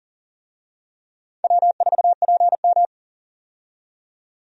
Morse code, heard as a keyed tone, about 40 wpm, 700 Hz. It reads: W4PM
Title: USA Call Signs - 4 Characters - 40wpm